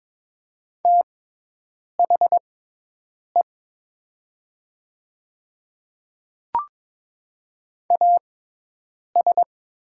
Morse code, heard as a keyed tone, about 22 wpm, spaced as for 8 wpm, 700 Hz. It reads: THE AS
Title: 6x Character Spacing - Top 100 Words - 22wpm